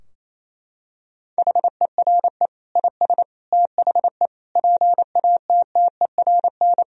Morse code, heard as a keyed tone, about 28 wpm, 700 Hz. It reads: HERE IS THE PATTERN
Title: Sentences from Top 1000 Words - Review Part 2 - 28wpm